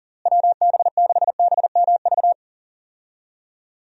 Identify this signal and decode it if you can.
Morse code, heard as a keyed tone, about 40 wpm, 700 Hz. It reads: WB6BMV
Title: USA Call Signs - 6 Characters - Repeated 2x - 40wpm